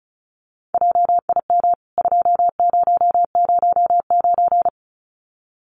Morse code, heard as a keyed tone, about 34 wpm, 700 Hz. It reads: JIM 2009
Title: CWT Exchange 35wpm